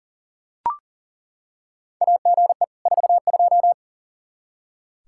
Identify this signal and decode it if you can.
Morse code, heard as a keyed tone, about 40 wpm, 700 Hz. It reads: AGE 42
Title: Practice QSO - Exchange 3 - More Info - 40wpm